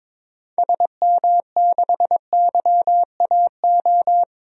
Morse code, heard as a keyed tone, about 21 wpm, 700 Hz. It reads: SM6YAO
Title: DXCC-100 Call Signs - 6 Characters - 22wpm